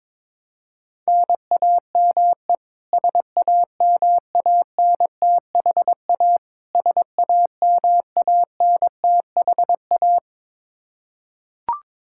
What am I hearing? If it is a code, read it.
Morse code, heard as a keyed tone, about 22 wpm, 700 Hz. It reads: NAME SAMANTHA SAMANTHA
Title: Practice QSO - Exchange 1 - Beginning - 22wpm